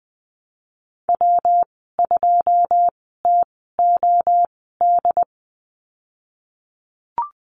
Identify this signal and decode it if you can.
Morse code, heard as a keyed tone, about 20 wpm, spaced as for 14 wpm, 700 Hz. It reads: W2TOD
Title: 140 Lesson 23g - Letter f - Call Signs 20wpm